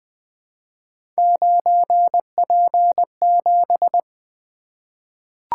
Morse code, heard as a keyed tone, about 20 wpm, 700 Hz. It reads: 9P7
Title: Three Letter-Number Combinations - 20wpm